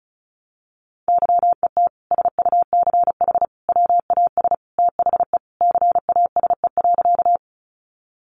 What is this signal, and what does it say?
Morse code, heard as a keyed tone, about 34 wpm, 700 Hz. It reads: YET SUCH WAS THE CASE.
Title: Sentences from Top 700 Words 35wpm